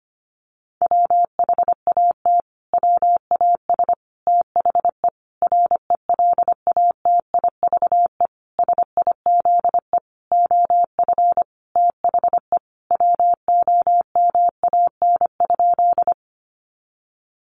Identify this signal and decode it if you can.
Morse code, heard as a keyed tone, about 25 wpm, 700 Hz. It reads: WHAT WAS THE RELATIVE SIZE OF THE WOMAN?
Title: Sentences from Top 1400 Words 25wpm